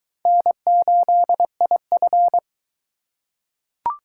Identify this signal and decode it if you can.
Morse code, heard as a keyed tone, about 22 wpm, 700 Hz. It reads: N8IF